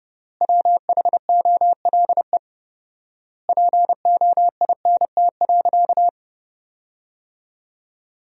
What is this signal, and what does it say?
Morse code, heard as a keyed tone, about 32 wpm, 700 Hz. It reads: WHOLE POINT.
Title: 4x Word-Spacing - Sentences from Top 400 Words - 30wpm